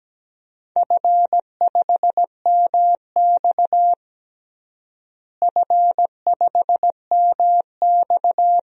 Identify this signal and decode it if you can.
Morse code, heard as a keyed tone, about 17 wpm, 700 Hz. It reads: F5MX F5MX
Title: DXCC-100 Call Signs - 4 Characters - Repeated 3x - 17wpm